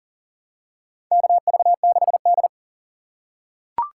Morse code, heard as a keyed tone, about 40 wpm, 700 Hz. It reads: KV6D
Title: USA Call Signs - 3x Character Spacing - 40wpm